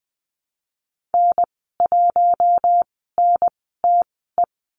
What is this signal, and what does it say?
Morse code, heard as a keyed tone, about 20 wpm, spaced as for 15 wpm, 700 Hz. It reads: N1NTE
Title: 128 Lesson 21g - Number 6 - Call Signs 20wpm